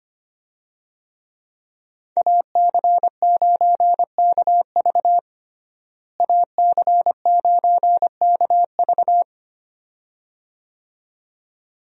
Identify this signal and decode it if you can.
Morse code, heard as a keyed tone, about 25 wpm, 700 Hz. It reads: AC9KV AC9KV
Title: USA Call Signs - 5 Characters - Repeated 2x - 25wpm